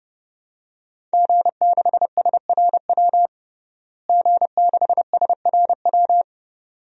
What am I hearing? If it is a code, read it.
Morse code, heard as a keyed tone, about 30 wpm, 700 Hz. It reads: G6SRW G6SRW
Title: DXCC-100 Call Signs - 5 Characters - Repeated 2x - 30wpm